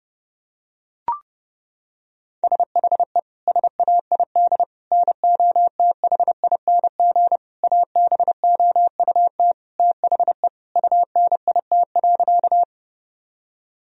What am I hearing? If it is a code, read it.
Morse code, heard as a keyed tone, about 30 wpm, 700 Hz. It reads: SHE SAID NOTHING ABOUT THE UNIT.
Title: Sentences from Top 1000 Words - Review Part 2 - 30wpm